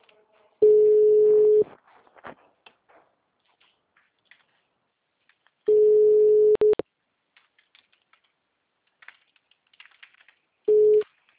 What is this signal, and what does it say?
Ringback tone (Europe, 425 Hz).
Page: Campo Limpo - SP